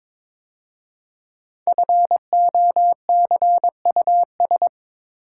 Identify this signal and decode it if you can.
Morse code, heard as a keyed tone, about 22 wpm, 700 Hz. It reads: FOCUS
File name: 5 Letter Words - First 3 Letters Given - 22wpm